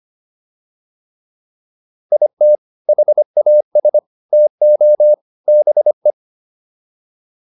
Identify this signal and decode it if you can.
Morse code, heard as a keyed tone, about 25 wpm, 600 Hz. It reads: IT HAS TO BE